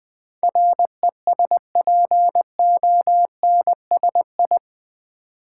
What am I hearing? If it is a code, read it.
Morse code, heard as a keyed tone, about 20 wpm, 700 Hz. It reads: RESPONSI
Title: Build A Word - 4 Letters At A Time - 11-Letter Words - 20wpm